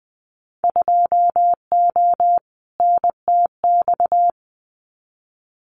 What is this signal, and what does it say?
Morse code, heard as a keyed tone, about 20 wpm, 700 Hz. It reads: 2O NTX
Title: Winter Field Day Exchanges Encore 20wpm